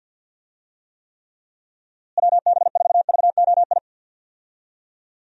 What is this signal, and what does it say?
Morse code, heard as a keyed tone, about 52 wpm, 700 Hz. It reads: WB4VOI
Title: USA Call Signs - 6 Characters - Repeated 3x - 50wpm